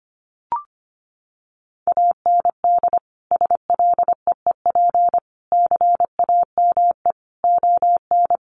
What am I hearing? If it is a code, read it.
Morse code, heard as a keyed tone, about 25 wpm, 700 Hz. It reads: AND SLEEP CAME ON
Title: Sentences from Top 1500 Words 25wpm